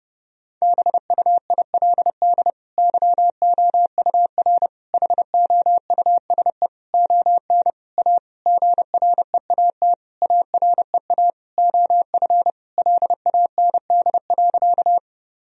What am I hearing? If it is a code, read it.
Morse code, heard as a keyed tone, about 30 wpm, 700 Hz. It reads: BUILD YOUR HOUSE ON A GREAT AREA OF LAND.
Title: Sentences from Top 1000 Words - Review Part 2 - 30wpm